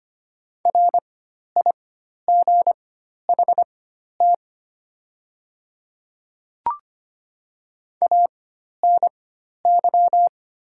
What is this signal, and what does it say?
Morse code, heard as a keyed tone, about 25 wpm, spaced as for 12 wpm, 700 Hz. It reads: RIGHT ANY
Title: 4x Character Spacing - Top 300 Words - 25wpm